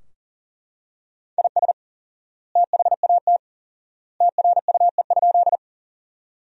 Morse code, heard as a keyed tone, about 40 wpm, 700 Hz. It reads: IS THAT TRUE?
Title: 4x Word-Spacing - Sentences from Top 400 Words - 40wpm